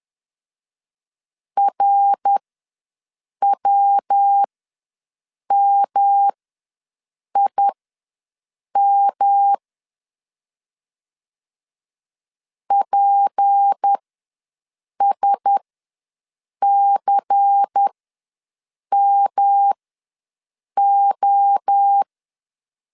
Morse code, heard as a keyed tone, about 10 wpm, spaced as for 6 wpm, 800 Hz. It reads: RWMIM PSCMO
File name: LEZIONI / 6 PKX0 / 6 PK ws 10wpm 3745